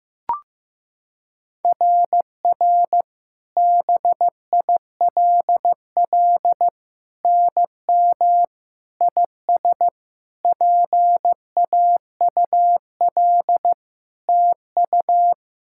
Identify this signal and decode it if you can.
Morse code, heard as a keyed tone, about 15 wpm, 700 Hz. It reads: RR BILL NM IS PAUL TU
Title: SOTA QSO Practice 15wpm